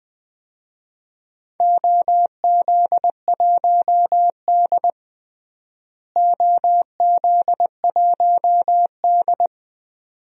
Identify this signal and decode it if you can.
Morse code, heard as a keyed tone, about 20 wpm, 700 Hz. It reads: OZ1D OZ1D
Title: DXCC-100 Call Signs - 4 Characters - Repeated 2x - 20wpm